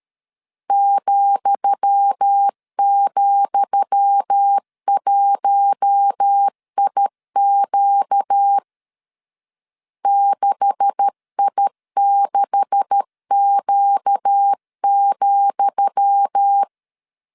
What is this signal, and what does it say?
Morse code, heard as a keyed tone, about 13 wpm, 800 Hz. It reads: ,,1IQ 6I6Q,
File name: LEZIONI / 10 SIGNS / 10D COMA 12wpm 2544